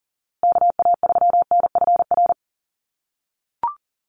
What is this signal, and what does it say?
Morse code, heard as a keyed tone, about 40 wpm, 700 Hz. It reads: KA3NFR
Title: USA Call Signs Repeated 3x 40wpm